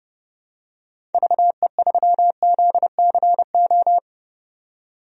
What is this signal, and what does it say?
Morse code, heard as a keyed tone, about 30 wpm, 700 Hz. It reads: VE3ZCO
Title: DXCC-100 Call Signs - 6 Characters - 30wpm